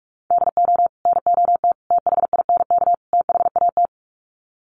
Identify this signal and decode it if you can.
Morse code, heard as a keyed tone, about 44 wpm, 700 Hz. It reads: DO NOT THINK THAT
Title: Sets of 4 Words 45wpm